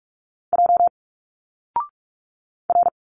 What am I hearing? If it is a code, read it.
Morse code, heard as a keyed tone, about 46 wpm, 700 Hz. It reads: J R
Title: Single Letters 45wpm